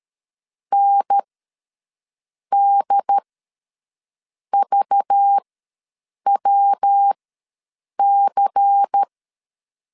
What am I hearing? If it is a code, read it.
Morse code, heard as a keyed tone, about 13 wpm, spaced as for 6 wpm, 800 Hz. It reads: NDVWC
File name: LEZIONI / 5 LBVC / 5 LBVC ws 12wpm 3159